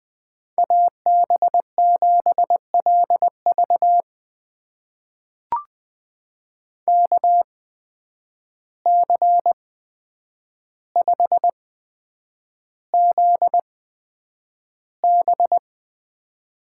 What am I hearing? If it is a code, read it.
Morse code, heard as a keyed tone, about 20 wpm, spaced as for 9 wpm, 700 Hz. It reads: AB7LV KC5ZB
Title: USA Call Signs - 5 Characters - 8x Character Spacing - 20wpm